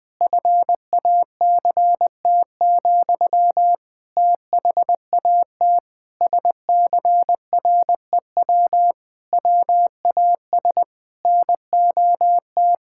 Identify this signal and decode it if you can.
Morse code, heard as a keyed tone, about 21 wpm, 700 Hz. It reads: FACT, THAT SCREW WAS NOT
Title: Sentences from Top 1600 Words 20wpm